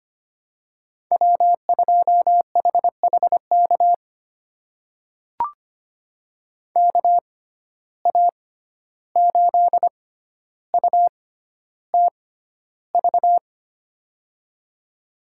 Morse code, heard as a keyed tone, about 25 wpm, spaced as for 12 wpm, 700 Hz. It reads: W2HHK KA8UTV
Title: USA Call Signs - 6x Character Spacing - 25wpm